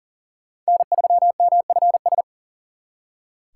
Morse code, heard as a keyed tone, about 52 wpm, 700 Hz. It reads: N3MFS